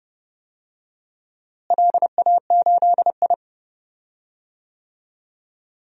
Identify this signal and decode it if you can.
Morse code, heard as a keyed tone, about 32 wpm, 700 Hz. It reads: LA8I
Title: DXCC-100 Call Signs - 4 Characters - Repeated 2x - 30wpm